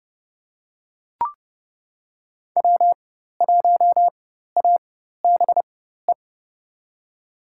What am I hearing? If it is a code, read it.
Morse code, heard as a keyed tone, about 30 wpm, spaced as for 14 wpm, 700 Hz. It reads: W1ABE
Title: USA Call Signs - 4x Character Spacing - 30wpm